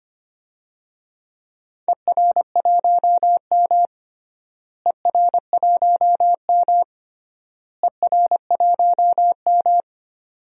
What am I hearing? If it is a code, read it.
Morse code, heard as a keyed tone, about 25 wpm, 700 Hz. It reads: ER1M ER1M ER1M
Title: DXCC-100 Call Signs - 4 Characters - Repeated 3x - 25wpm